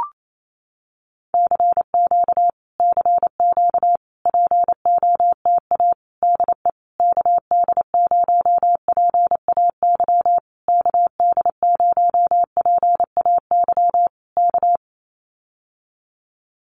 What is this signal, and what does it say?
Morse code, heard as a keyed tone, about 28 wpm, 700 Hz. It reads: CQ CQ POTA DE KD0PAY KD0PAY K
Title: Common QSO Phrases Encore 28wpm